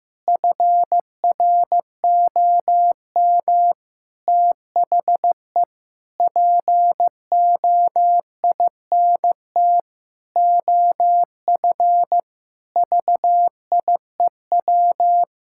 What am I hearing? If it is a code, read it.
Morse code, heard as a keyed tone, about 15 wpm, 700 Hz. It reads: FROM THE POINT OF VIEW
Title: Sentences from Top 1000 Words - Review Part 6 - 15wpm